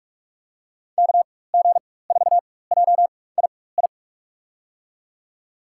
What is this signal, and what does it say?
Morse code, heard as a keyed tone, about 44 wpm, spaced as for 21 wpm, 700 Hz. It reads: KG4JII
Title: USA Call Signs - 6 Characters - 4x Character Spacing - 45wpm